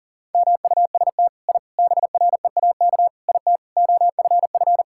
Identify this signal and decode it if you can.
Morse code, heard as a keyed tone, about 40 wpm, 700 Hz. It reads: MUST I BREAK IT OFF
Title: Sentences from Top 1000 Words - Review Part 6 - 40wpm